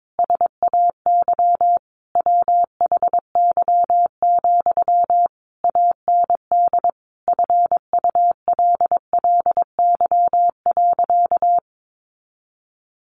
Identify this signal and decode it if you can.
Morse code, heard as a keyed tone, about 22 wpm, 700 Hz. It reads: SAY WHY, AND FULLY.
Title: Sentences from Top 1300 Words 22wpm